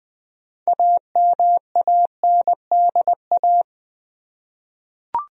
Morse code, heard as a keyed tone, about 21 wpm, 700 Hz. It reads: AMANDA
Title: USA Common Names - 20wpm